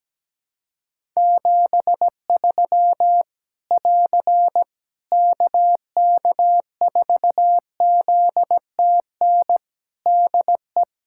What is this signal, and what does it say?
Morse code, heard as a keyed tone, about 17 wpm, 700 Hz. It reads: 73 <AR> KK4ZTN DE
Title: K1USN SST QSO Practice 17wpm